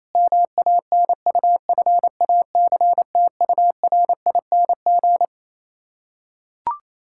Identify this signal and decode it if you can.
Morse code, heard as a keyed tone, about 28 wpm, 700 Hz. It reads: MANUFACTURING
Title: K1USN SST QSO Practice 28wpm